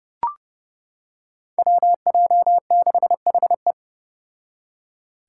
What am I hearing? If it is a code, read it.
Morse code, heard as a keyed tone, about 32 wpm, 700 Hz. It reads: WJ6HE
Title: Five Challenging Letter-Number Combinations 30wpm